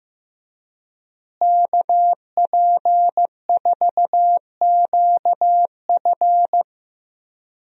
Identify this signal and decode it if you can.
Morse code, heard as a keyed tone, about 15 wpm, 700 Hz. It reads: KP4QF
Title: DXCC-100 Call Signs - 5 Characters - Repeated 2x - 15wpm